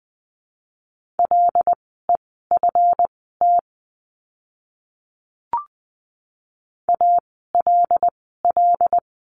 Morse code, heard as a keyed tone, about 20 wpm, spaced as for 15 wpm, 700 Hz. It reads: LEFT ALL
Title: 139 Lesson 23f - Letter f - Words 20wpm